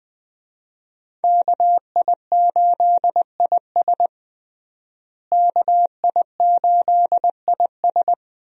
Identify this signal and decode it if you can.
Morse code, heard as a keyed tone, about 21 wpm, 700 Hz. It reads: KI8IS KI8IS
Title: USA Call Signs - 5 Characters - Repeated 3x - 20wpm